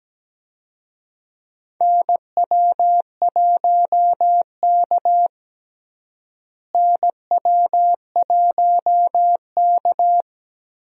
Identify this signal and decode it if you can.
Morse code, heard as a keyed tone, about 17 wpm, 700 Hz. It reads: NW1K NW1K
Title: USA Call Signs - 4 Characters - Repeated 3x - 17wpm